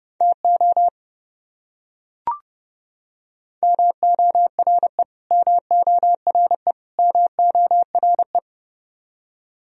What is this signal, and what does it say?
Morse code, heard as a keyed tone, about 30 wpm, 700 Hz. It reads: TO MORE MORE MORE
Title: Top 100 Words Repeated 30wpm